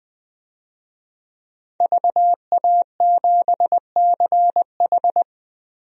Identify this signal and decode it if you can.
Morse code, heard as a keyed tone, about 20 wpm, 700 Hz. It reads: VA7CH
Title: DXCC-100 Call Signs - 5 Characters - Repeated 2x - 20wpm